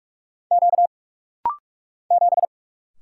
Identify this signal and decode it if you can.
Morse code, heard as a keyed tone, about 46 wpm, 700 Hz. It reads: Q 7
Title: Single Letter-Number - Rapid-Fire 45wpm